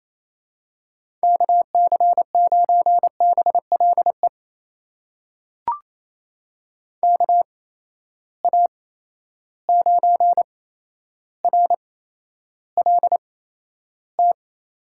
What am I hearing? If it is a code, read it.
Morse code, heard as a keyed tone, about 28 wpm, spaced as for 12 wpm, 700 Hz. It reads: KC9BLE KA9RLT
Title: USA Call Signs - 8x Character Spacing - 28wpm